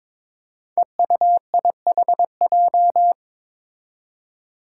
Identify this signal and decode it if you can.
Morse code, heard as a keyed tone, about 22 wpm, 700 Hz. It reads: EUIHJ
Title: Five Challenging Letter-Number Combinations 22wpm